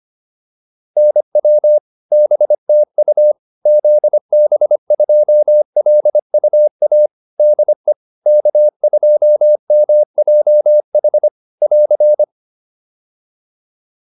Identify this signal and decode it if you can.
Morse code, heard as a keyed tone, about 25 wpm, 600 Hz. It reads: NW BTU ZB2LUA DE K2MJH <AR>